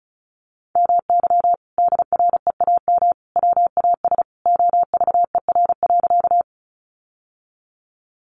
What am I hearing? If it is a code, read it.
Morse code, heard as a keyed tone, about 34 wpm, 700 Hz. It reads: MY DREAM WAS OVER.
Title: Sentences from Top 1500 Words 35wpm